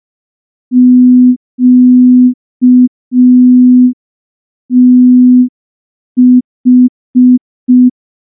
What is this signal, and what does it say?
Morse code, heard as a keyed tone, about 5 wpm, 250 Hz. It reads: QTH
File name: morse / morse-training-05WPM